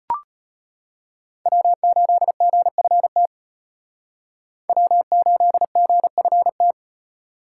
Morse code, heard as a keyed tone, about 32 wpm, 700 Hz. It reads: W8GFT W8GFT